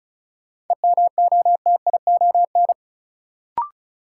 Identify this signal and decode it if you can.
Morse code, heard as a keyed tone, about 36 wpm, 700 Hz. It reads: EMOTION
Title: Sentences from Top 1600 Words - New Word Drill - 35wpm